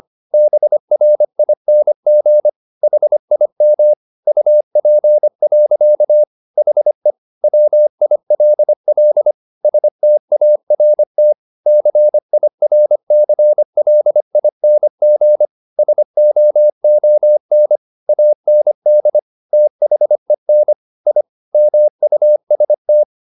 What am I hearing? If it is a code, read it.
Morse code, heard as a keyed tone, about 25 wpm, 600 Hz. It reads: BRING HIM UP. HE WILL START CIRCLING SOON AND THEN I MUST